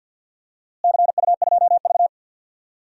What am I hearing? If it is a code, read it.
Morse code, heard as a keyed tone, about 52 wpm, 700 Hz. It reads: CU2V
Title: DXCC-100 Call Signs - 4 Characters - Repeated 2x - 50wpm